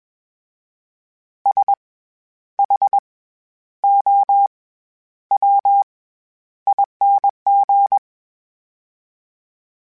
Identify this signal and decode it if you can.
Morse code, heard as a keyed tone, about 21 wpm, spaced as for 10 wpm, 800 Hz. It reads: SHOWING
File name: Special INGs-mp3s / Special INGs-mp3s / SlowToFaster ing @20